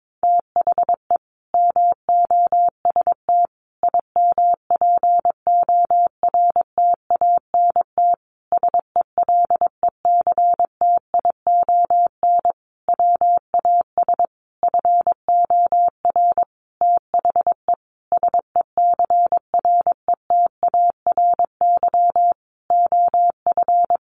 Morse code, heard as a keyed tone, about 22 wpm, 700 Hz. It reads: THE MOST IMPORTANT SELECTION WAS FOR THE SECRETARY OF
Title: Sentences from Top 1400 Words 22wpm